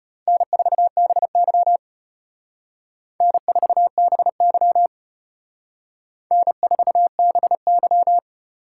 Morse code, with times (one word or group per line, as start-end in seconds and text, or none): 0.3-1.8 N4BY
3.2-4.9 N4BY
6.3-8.2 N4BY